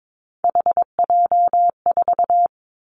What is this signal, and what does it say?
Morse code, heard as a keyed tone, about 22 wpm, 700 Hz. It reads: HJ4
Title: Callsign Prefixes for Top 100 DXCC - 22wpm